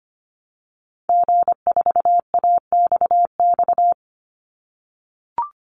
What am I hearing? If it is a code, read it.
Morse code, heard as a keyed tone, about 25 wpm, 700 Hz. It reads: G4AXX
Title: Call Signs - 25wpm